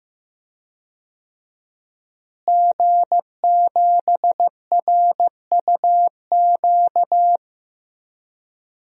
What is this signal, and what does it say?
Morse code, heard as a keyed tone, about 15 wpm, 700 Hz. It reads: G7RUQ
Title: DXCC-100 Call Signs - 5 Characters - Repeated 2x - 15wpm